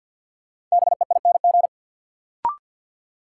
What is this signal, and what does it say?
Morse code, heard as a keyed tone, about 50 wpm, 700 Hz. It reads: BEING
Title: Sets of 4 Words - New Word Drill - 50wpm